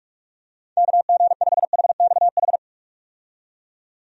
Morse code, heard as a keyed tone, about 46 wpm, 700 Hz. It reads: KG5HXH
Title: USA Call Signs - 6 Characters - 45wpm